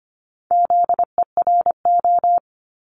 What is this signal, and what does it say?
Morse code, heard as a keyed tone, about 24 wpm, 700 Hz. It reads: ZERO
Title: Multi-letter Phonemes 25wpm